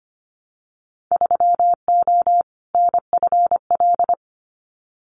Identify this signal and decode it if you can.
Morse code, heard as a keyed tone, about 25 wpm, 700 Hz. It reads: 3O NFL
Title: Winter Field Day Exchanges 25wpm